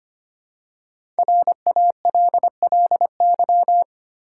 Morse code, heard as a keyed tone, about 26 wpm, 700 Hz. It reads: RALLY
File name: 5 Letter Words - First 3 Letters Given - 25wpm